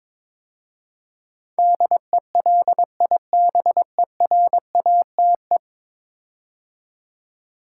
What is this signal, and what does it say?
Morse code, heard as a keyed tone, about 22 wpm, 700 Hz. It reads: DELIBERATE
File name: practice / 22wpm / 10 Letter Words - First 3 Letters Given - 22wpm